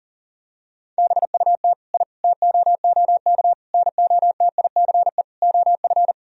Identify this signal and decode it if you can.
Morse code, heard as a keyed tone, about 40 wpm, 700 Hz. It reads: BUT I TOOK NOTICE OF
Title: Sentences from Top 1000 Words - Review Part 2 - 40wpm